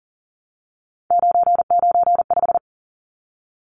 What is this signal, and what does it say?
Morse code, heard as a keyed tone, about 40 wpm, 700 Hz. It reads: 995
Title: Numbers - 3-Digit 40wpm